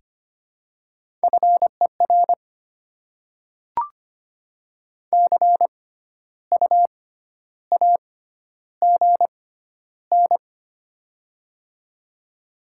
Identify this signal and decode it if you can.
Morse code, heard as a keyed tone, about 25 wpm, spaced as for 12 wpm, 700 Hz. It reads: FER CU AGN
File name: Top 20 QSO Elements - 6x Character Spacing - 25wpm